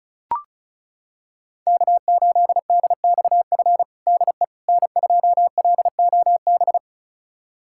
Morse code, heard as a keyed tone, about 36 wpm, 700 Hz. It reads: K8DXF DE N2LOB
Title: USA Callsign DE Callsign - 5 Characters - 35wpm